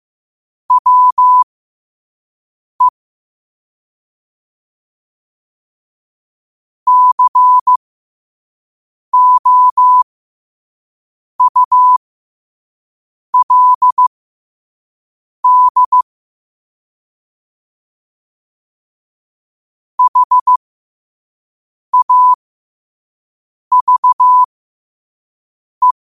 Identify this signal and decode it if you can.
Morse code, heard as a keyed tone, about 15 wpm, spaced as for 5 wpm, 1000 Hz. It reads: WE COULD HAVE
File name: qotdfiles / 5wpm